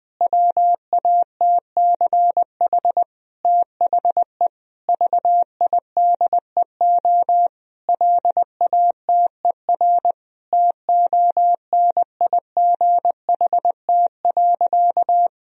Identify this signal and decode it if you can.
Morse code, heard as a keyed tone, about 21 wpm, 700 Hz. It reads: WATCH THE VIDEO LATER TONIGHT.
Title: Sentences from Top 1000 Words - Review Part 4 - 20wpm